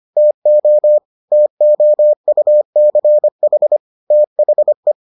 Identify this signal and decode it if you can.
Morse code, heard as a keyed tone, about 25 wpm, 600 Hz. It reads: TO TOUCH THE